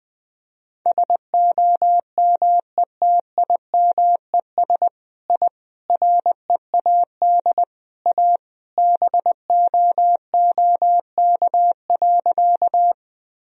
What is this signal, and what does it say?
Morse code, heard as a keyed tone, about 20 wpm, 700 Hz. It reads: SOMETIMES I READ A BOOK.
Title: Sentences from Top 1000 Words - Review Part 3 - 20wpm